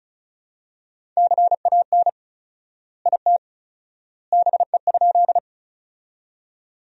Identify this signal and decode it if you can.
Morse code, heard as a keyed tone, about 36 wpm, 700 Hz. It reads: CAN IT BE?
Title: 4x Word-Spacing - Sentences from Top 100 Words - 35wpm